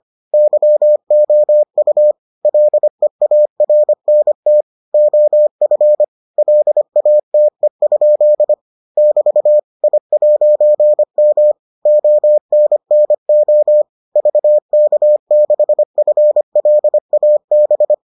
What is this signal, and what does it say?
Morse code, heard as a keyed tone, about 25 wpm, 600 Hz. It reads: YOU LEARNT OF LATE? <BT> I'M ONNO VK6FLAB